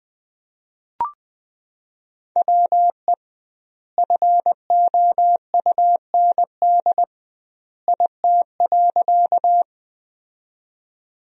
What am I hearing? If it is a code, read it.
Morse code, heard as a keyed tone, about 20 wpm, 700 Hz. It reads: WE FOUND IT.
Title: 2x Word-Spacing - Sentences from Top 300 Words - 20wpm